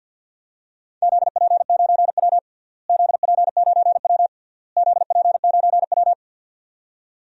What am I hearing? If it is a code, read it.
Morse code, heard as a keyed tone, about 50 wpm, 700 Hz. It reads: ZP9W ZP9W ZP9W
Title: DXCC-100 Call Signs - 4 Characters - Repeated 3x - 50wpm